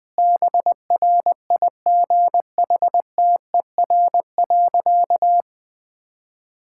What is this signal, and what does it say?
Morse code, heard as a keyed tone, about 20 wpm, 700 Hz. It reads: BRIGHTER.
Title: Difficult Sentences with Hints - 20wpm